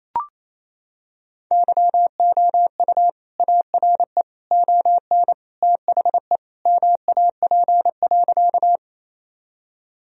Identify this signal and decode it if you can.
Morse code, heard as a keyed tone, about 28 wpm, 700 Hz. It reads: YOU ARE ON THE MAP.
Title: Sentences from Top 1000 Words - Review Part 2 - 28wpm